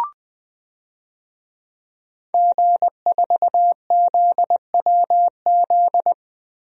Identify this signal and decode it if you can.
Morse code, heard as a keyed tone, about 21 wpm, 700 Hz. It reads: G4ZWZ
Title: DXCC-100 Call Signs - 5 Characters - 20wpm